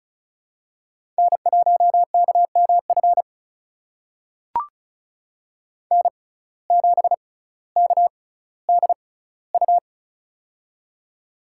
Morse code, heard as a keyed tone, about 34 wpm, spaced as for 18 wpm, 700 Hz. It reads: N1KMF N7KDU
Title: USA Call Signs - 5 Characters - 6x Character Spacing - 35wpm